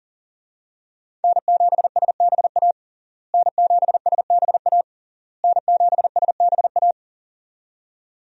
Morse code, heard as a keyed tone, about 40 wpm, 700 Hz. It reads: N7SBA N7SBA N7SBA
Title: USA Call Signs - 5 Characters - Repeated 3x - 40wpm